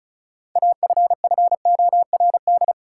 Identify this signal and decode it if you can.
Morse code, heard as a keyed tone, about 36 wpm, 700 Hz. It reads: AFFORD
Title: Six Letter Words - 35wpm